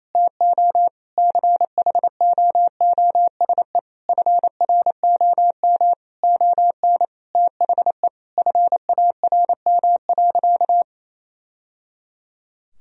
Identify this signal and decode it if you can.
Morse code, heard as a keyed tone, about 28 wpm, 700 Hz. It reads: TO CHOOSE FROM ON THE FARM.
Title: Sentences from Top 1000 Words - Review Part 4 - 28wpm